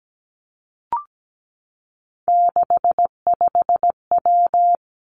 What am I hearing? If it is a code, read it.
Morse code, heard as a keyed tone, about 17 wpm, 700 Hz. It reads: 65W
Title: Three Challenging Letter-Number Combinations 17wpm